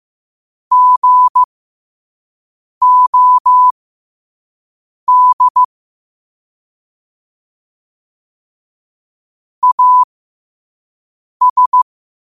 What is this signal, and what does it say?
Morse code, heard as a keyed tone, about 15 wpm, spaced as for 5 wpm, 1000 Hz. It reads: GOD AS
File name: qotdfiles / 5wpm